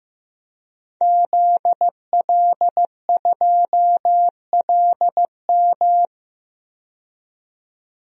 Morse code, heard as a keyed tone, about 15 wpm, 700 Hz. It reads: ZL2LM
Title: DXCC-100 Call Signs - 5 Characters - 15wpm